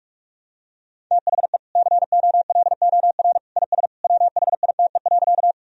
Morse code, heard as a keyed tone, about 41 wpm, 700 Hz. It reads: THE COLOR IS WHITE.
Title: Sentences from Top 1000 Words - Review Part 2 - 45wpm